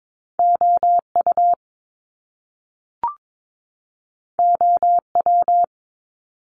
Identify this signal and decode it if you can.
Morse code, heard as a keyed tone, about 22 wpm, 700 Hz. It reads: OU OW
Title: Top Two Letter Combinations in Words 22wpm